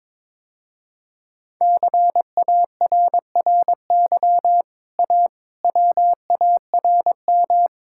Morse code, heard as a keyed tone, about 22 wpm, 700 Hz. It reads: CARRY A WARM
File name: practice / 22wpm / Easy Sentences with Hints - 22wpm